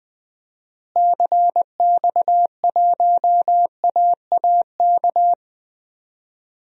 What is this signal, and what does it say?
Morse code, heard as a keyed tone, about 20 wpm, 700 Hz. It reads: CX1AAK
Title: DXCC-100 Call Signs - 6 Characters - Repeated 2x - 20wpm